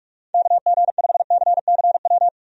Morse code, heard as a keyed tone, about 41 wpm, 700 Hz. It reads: KG5CCW
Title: USA Call Signs - 6 Characters - Repeated 3x - 45wpm